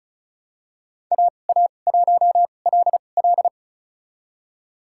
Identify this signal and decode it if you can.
Morse code, heard as a keyed tone, about 34 wpm, spaced as for 25 wpm, 700 Hz. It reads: AA1LL
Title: USA Call Signs - 5 Characters - 2x Character Spacing - 35wpm